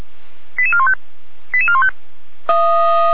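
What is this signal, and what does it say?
Five-tone selective call (ZVEI), heard as quick runs of five tones, tones of about 70 ms. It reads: 88415 88415
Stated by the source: ZVEI five-tone selective call received by SDR.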